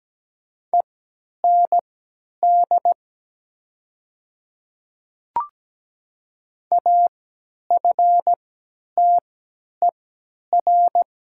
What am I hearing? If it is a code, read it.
Morse code, heard as a keyed tone, about 17 wpm, spaced as for 10 wpm, 700 Hz. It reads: END AFTER
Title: 3x Character Spacing - Top 300 Words - 17wpm